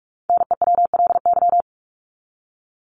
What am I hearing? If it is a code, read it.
Morse code, heard as a keyed tone, about 44 wpm, 700 Hz. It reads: NEWLY
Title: LY Suffix - Top 100 Words with Base Words 45wpm